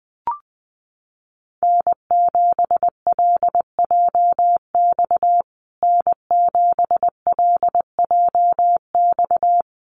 Morse code, heard as a keyed tone, about 20 wpm, 700 Hz. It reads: N7LJX N7LJX
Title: USA Call Signs Repeated 2x 20wpm